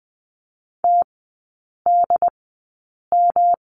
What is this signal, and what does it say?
Morse code, heard as a keyed tone, about 20 wpm, spaced as for 8 wpm, 700 Hz. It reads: TDM
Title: 127 Lesson 21e - Number 6 - Three Characters 20wpm